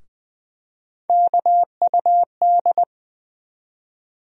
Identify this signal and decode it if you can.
Morse code, heard as a keyed tone, about 20 wpm, 700 Hz. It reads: KUD
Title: USA Call Sign Postfixes - 3 Letters 20wpm